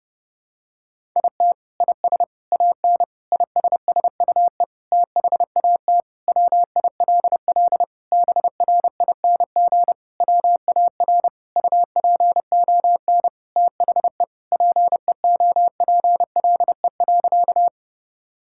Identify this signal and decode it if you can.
Morse code, heard as a keyed tone, about 32 wpm, 700 Hz. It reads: IT IS AN ISSUE THAT WILL BRING WAR UPON THE PEOPLE.
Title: Sentences from Top 1000 Words - Review Part 6 - 30wpm